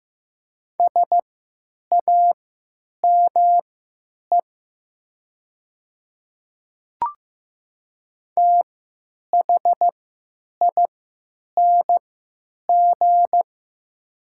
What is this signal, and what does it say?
Morse code, heard as a keyed tone, about 15 wpm, spaced as for 9 wpm, 700 Hz. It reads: SAME THING
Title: 3x Character Spacing - Top 200 Words - 15wpm